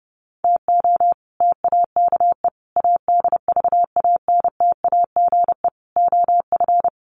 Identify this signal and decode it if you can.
Morse code, heard as a keyed tone, about 30 wpm, 700 Hz. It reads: TO TAKE ADVANTAGE OF
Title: Sets of 4 Words 30wpm